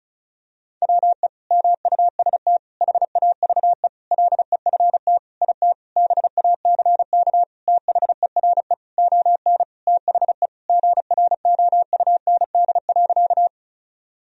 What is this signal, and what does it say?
Morse code, heard as a keyed tone, about 36 wpm, 700 Hz. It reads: WE MUST HAVE LEFT IT BACK THERE ON THE GROUND.
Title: Sentences from Top 1000 Words - Review Part 2 - 35wpm